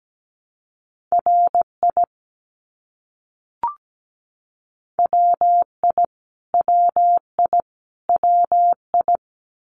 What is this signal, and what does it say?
Morse code, heard as a keyed tone, about 17 wpm, 700 Hz. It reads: RI WI WI WI
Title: US States Repeated 3x 17wpm